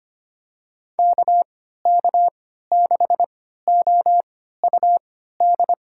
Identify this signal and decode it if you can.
Morse code, heard as a keyed tone, about 25 wpm, spaced as for 14 wpm, 700 Hz. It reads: KK6OUD
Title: USA Call Signs - 6 Characters - 3x Character Spacing - 25wpm